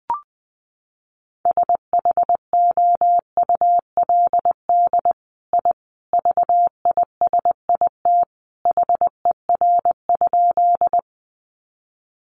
Morse code, heard as a keyed tone, about 20 wpm, 700 Hz. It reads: SHOULD I VISIT HER?
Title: Sentences from Top 900 Words 20wpm